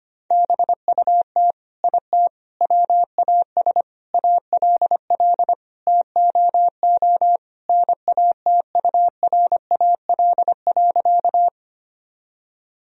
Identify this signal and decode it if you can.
Morse code, heard as a keyed tone, about 25 wpm, 700 Hz. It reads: BUT IT WAS ALL TOO NATURAL.
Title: Sentences from Top 1000 Words - Review Part 4 - 25wpm